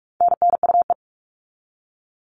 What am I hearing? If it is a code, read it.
Morse code, heard as a keyed tone, about 44 wpm, 700 Hz. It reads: NNUE
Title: All Cut Numbers - 4-Digit 45wpm